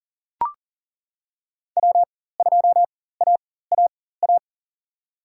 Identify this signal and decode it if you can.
Morse code, heard as a keyed tone, about 40 wpm, spaced as for 19 wpm, 700 Hz. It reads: W2AAA
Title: USA Call Signs - 5 Characters - 4x Character Spacing - 40wpm